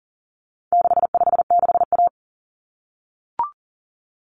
Morse code, heard as a keyed tone, about 38 wpm, 700 Hz. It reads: 656A
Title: Common Cut Numbers - 4-Digit 40wpm